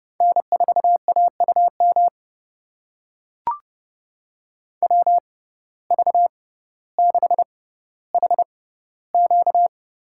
Morse code, heard as a keyed tone, about 30 wpm, spaced as for 15 wpm, 700 Hz. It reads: N4AUM WV6HQ
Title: USA Call Signs - 5 Characters - 6x Character Spacing - 30wpm